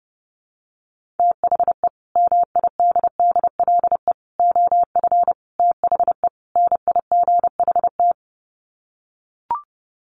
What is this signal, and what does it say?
Morse code, heard as a keyed tone, about 30 wpm, 700 Hz. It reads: THE MIDDLE OF THE NIGHT
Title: Sets of 5 Words 30wpm